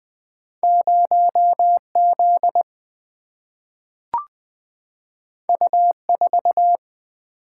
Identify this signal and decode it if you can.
Morse code, heard as a keyed tone, about 20 wpm, 700 Hz. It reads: 0 Z U 4
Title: Single-Letter Single-Number 20wpm